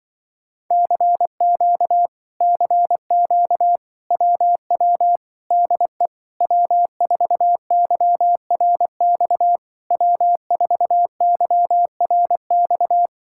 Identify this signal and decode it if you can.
Morse code, heard as a keyed tone, about 24 wpm, 700 Hz. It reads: CQ CQ WW DE W4YRX W4YRX
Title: Speed-Racing Common QSO Phrases - 20wpm